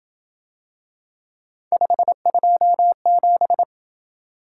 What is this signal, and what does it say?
Morse code, heard as a keyed tone, about 30 wpm, 700 Hz. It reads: 527